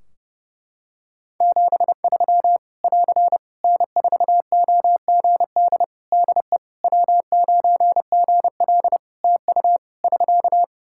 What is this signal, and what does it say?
Morse code, heard as a keyed tone, about 32 wpm, 700 Hz. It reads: 73 <AR> N4OGD DE W9GL TU <SK>
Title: K1USN SST QSO Practice 30wpm